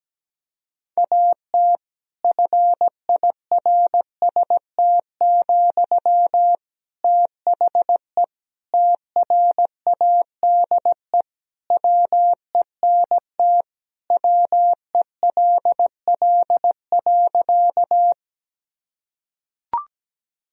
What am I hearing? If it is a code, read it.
Morse code, heard as a keyed tone, about 17 wpm, 700 Hz. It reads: AT FIRST, THE TRADE WENT WELL.
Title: Sentences from Top 1000 Words - Review Part 4 - 17wpm